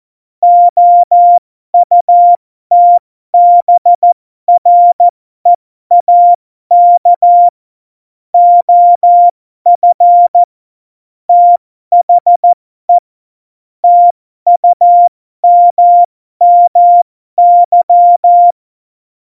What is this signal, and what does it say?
Morse code, heard as a keyed tone, about 14 wpm, spaced as for 12 wpm, 700 Hz. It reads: OUTBREAK OF THE TUMMY
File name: Morsekurs / Witze